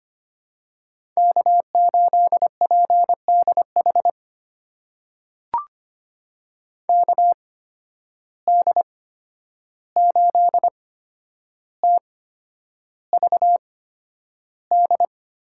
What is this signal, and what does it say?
Morse code, heard as a keyed tone, about 25 wpm, spaced as for 10 wpm, 700 Hz. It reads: K8PDH KD8TVD
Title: USA Call Signs - 8x Character Spacing - 25wpm